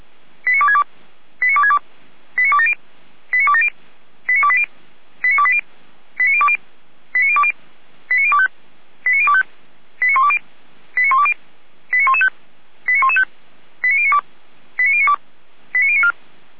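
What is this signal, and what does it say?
Five-tone selective call (ZVEI), heard as quick runs of five tones, tones of about 70 ms. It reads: 89262 89262 89270 89270 89280 89280 89020 89020 89025 89025 89120 89120 89115 89115 89092 89092 89094
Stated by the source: A ZVEI five-tone selective call received by SDR.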